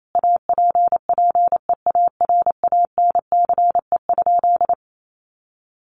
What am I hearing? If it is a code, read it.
Morse code, heard as a keyed tone, about 28 wpm, 700 Hz. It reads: APPEARANCE?
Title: Sentences from Top 1100 Words 28wpm